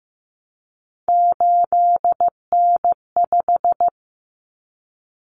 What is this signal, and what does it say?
Morse code, heard as a keyed tone, about 15 wpm, 700 Hz. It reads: 8N5
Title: Common Cut Numbers - 3-Digit 15wpm